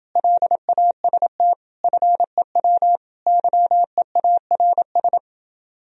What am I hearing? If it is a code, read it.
Morse code, heard as a keyed tone, about 29 wpm, 700 Hz. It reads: LAST FEW YEARS